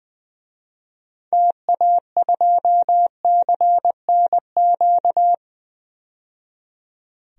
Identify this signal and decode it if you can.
Morse code, heard as a keyed tone, about 20 wpm, 700 Hz. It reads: TA2CNQ
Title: DXCC-100 Call Signs - 6 Characters - 20wpm